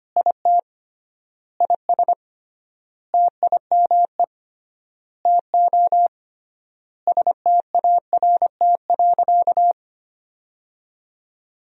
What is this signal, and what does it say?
Morse code, heard as a keyed tone, about 25 wpm, 700 Hz. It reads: IT IS TIME TO START.
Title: 3x Word-Spacing - Sentences from Top 300 Words - 25wpm